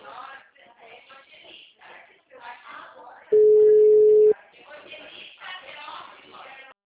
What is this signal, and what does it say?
Ringback tone (Europe, 425 Hz), with speech.
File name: out- -2003- - - .18659